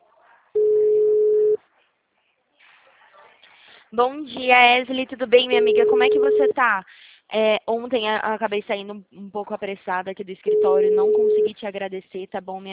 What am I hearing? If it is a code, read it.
Ringback tone (Europe, 425 Hz), with speech.
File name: out- -2009- - - .22